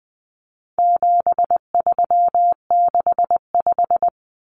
Morse code, heard as a keyed tone, about 20 wpm, 700 Hz. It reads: 7365
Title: Common Cut Numbers - 4-Digit 20wpm